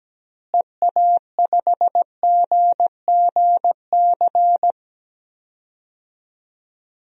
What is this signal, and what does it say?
Morse code, heard as a keyed tone, about 17 wpm, 700 Hz. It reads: EA5GGC
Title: DXCC-100 Call Signs - 6 Characters - Repeated 2x - 17wpm